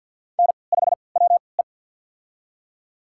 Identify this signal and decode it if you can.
Morse code, heard as a keyed tone, about 51 wpm, spaced as for 28 wpm, 700 Hz. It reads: N5WE
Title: USA Call Signs - 4 Characters - 3x Character Spacing - 50wpm